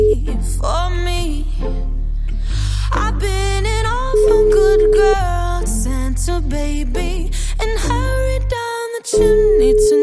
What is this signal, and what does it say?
Ringback tone (Europe, 425 Hz).